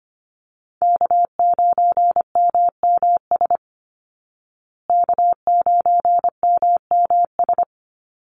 Morse code, heard as a keyed tone, about 25 wpm, 700 Hz. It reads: K9MMS K9MMS
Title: Call Signs Repeated 2x - 25wpm